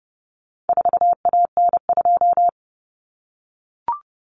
Morse code, heard as a keyed tone, about 30 wpm, 700 Hz. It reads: 4AN2
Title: Common Cut Numbers - 4-Digit 30wpm